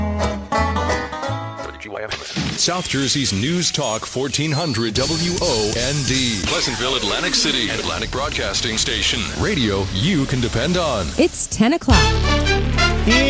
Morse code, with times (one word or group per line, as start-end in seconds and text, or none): none